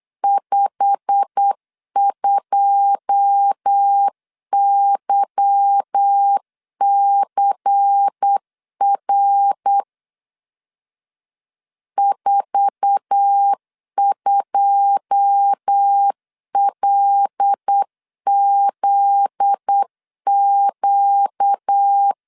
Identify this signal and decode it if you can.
Morse code, heard as a keyed tone, about 8 wpm, 800 Hz. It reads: 52YCR 42LZQ
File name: LEZIONI / 9 2 8wpm 3437